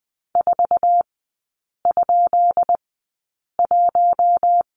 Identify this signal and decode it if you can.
Morse code, heard as a keyed tone, about 20 wpm, spaced as for 8 wpm, 700 Hz. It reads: 4?1
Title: 178 Lesson 29e - Slash - Three Characters 20wpm